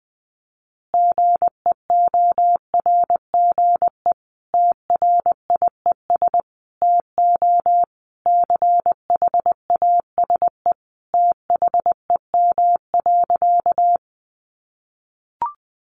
Morse code, heard as a keyed tone, about 20 wpm, 700 Hz. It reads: GEORGE TRIES TO CHASE THEM.
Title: Additional Easy Phrases 20wpm